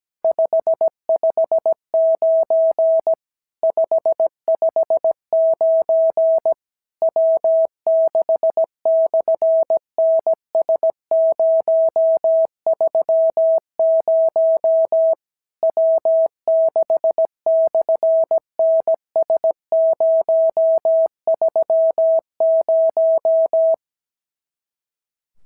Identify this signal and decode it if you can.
Morse code, heard as a keyed tone, about 17 wpm, 650 Hz. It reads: 559 559 W6/NS030 W6/NS030
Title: SOTA QSO Practice 17wpm